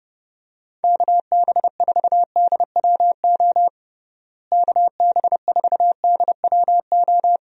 Morse code, heard as a keyed tone, about 30 wpm, 700 Hz. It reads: KB4DWO KB4DWO
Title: USA Call Signs - 6 Characters - Repeated 2x - 30wpm